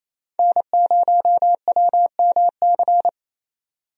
Morse code, heard as a keyed tone, about 27 wpm, 700 Hz. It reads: N0WMC
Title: USA Call Signs - 5 Characters - 28wpm